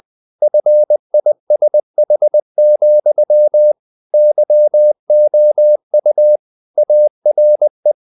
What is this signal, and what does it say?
Morse code, heard as a keyed tone, about 20 wpm, 600 Hz. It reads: FISH, YOU ARE
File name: the old man and the sea part 2 20wpm